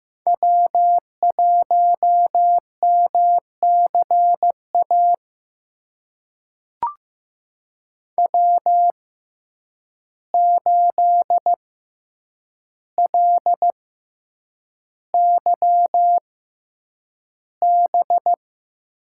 Morse code, heard as a keyed tone, about 15 wpm, spaced as for 8 wpm, 700 Hz. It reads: W1MCA W8LYB
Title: USA Call Signs - 6x Character Spacing - 15wpm